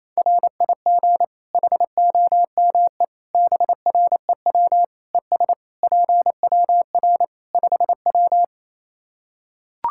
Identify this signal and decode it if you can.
Morse code, heard as a keyed tone, about 27 wpm, 700 Hz. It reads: RIG HOME BREW ES PWR 5W
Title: K1USN SST QSO Practice 28wpm